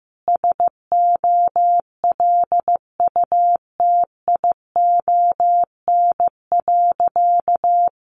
Morse code, heard as a keyed tone, about 15 wpm, 700 Hz. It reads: SOLUTION.
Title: Sentences from Top 1300 Words 15wpm